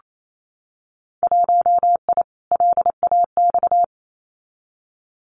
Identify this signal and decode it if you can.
Morse code, heard as a keyed tone, about 28 wpm, 700 Hz. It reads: 1I LAX
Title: Winter Field Day Exchanges 28wpm